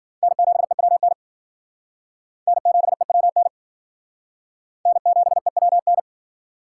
Morse code, heard as a keyed tone, about 50 wpm, 700 Hz. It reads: N7EWN N7EWN N7EWN
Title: Speed Racing - USA 1x3 Call Signs 40wpm